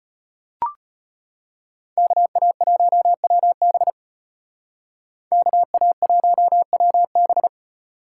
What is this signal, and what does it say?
Morse code, heard as a keyed tone, about 30 wpm, 700 Hz. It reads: KA1WB KA1WB
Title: Speed Racing - USA 2x2 Call Signs 25wpm